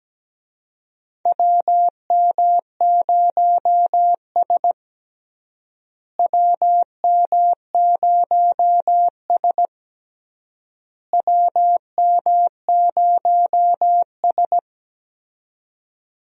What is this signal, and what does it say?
Morse code, heard as a keyed tone, about 17 wpm, 700 Hz. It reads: WM0S WM0S WM0S
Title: USA Call Signs - 4 Characters - Repeated 3x - 17wpm